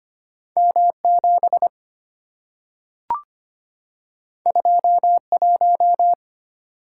Morse code, heard as a keyed tone, about 25 wpm, 700 Hz. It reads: M7 21
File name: Two Letter-Number Combinations - 25wpm